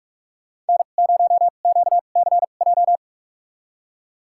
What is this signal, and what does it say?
Morse code, heard as a keyed tone, about 46 wpm, spaced as for 32 wpm, 700 Hz. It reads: N0QCJ
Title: USA Call Signs - 5 Characters - 2x Character Spacing - 45wpm